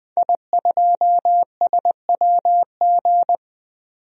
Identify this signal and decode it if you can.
Morse code, heard as a keyed tone, about 20 wpm, 700 Hz. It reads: I2SWG
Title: DXCC-100 Call Signs - 5 Characters - 20wpm